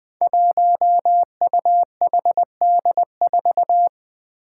Morse code, heard as a keyed tone, about 20 wpm, 700 Hz. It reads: 1UHD4
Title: Five Challenging Letter-Number Combinations 20wpm